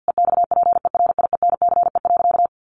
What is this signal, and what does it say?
Morse code, heard as a keyed tone, about 50 wpm, 700 Hz. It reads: EXPERIENCE.
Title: Sentences from Top 1300 Words 50wpm